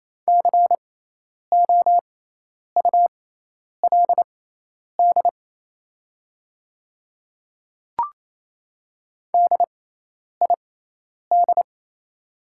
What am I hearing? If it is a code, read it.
Morse code, heard as a keyed tone, about 28 wpm, spaced as for 10 wpm, 700 Hz. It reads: COULD DID
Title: 6x Character Spacing - Top 300 Words - 28wpm